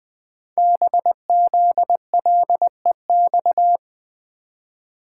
Morse code, heard as a keyed tone, about 20 wpm, 700 Hz. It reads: BZLEX
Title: Code Groups - 5 Letters - 20wpm